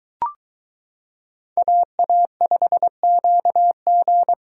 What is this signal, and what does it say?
Morse code, heard as a keyed tone, about 22 wpm, 700 Hz. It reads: AA5QG